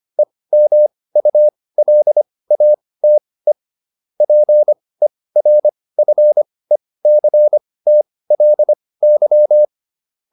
Morse code, heard as a keyed tone, about 25 wpm, spaced as for 18 wpm, 600 Hz. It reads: EMULATE PERFECTLY